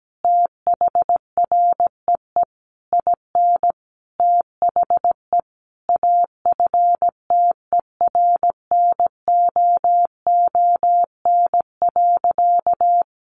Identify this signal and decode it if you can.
Morse code, heard as a keyed tone, about 17 wpm, 700 Hz. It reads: THREE IN THE AFTERNOON.
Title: Sentences from Top 1500 Words 17wpm